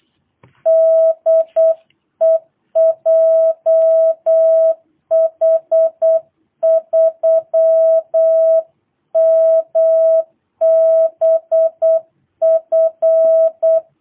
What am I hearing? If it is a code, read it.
Morse code, heard as a keyed tone, about 8 wpm, 660 Hz. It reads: DE JH3MBF
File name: jh8cbh / 2022cw